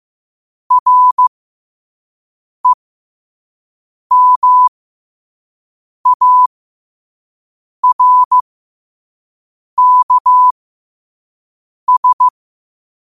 Morse code, heard as a keyed tone, about 15 wpm, spaced as for 5 wpm, 1000 Hz. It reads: REMARKS
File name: qotdfiles / 5wpm